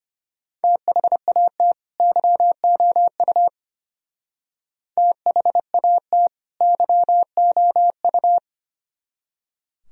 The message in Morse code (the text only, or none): THAT YOU THAT YOU